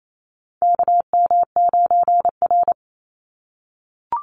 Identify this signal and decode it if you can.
Morse code, heard as a keyed tone, about 28 wpm, 700 Hz. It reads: KM9R
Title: Call Signs - 28wpm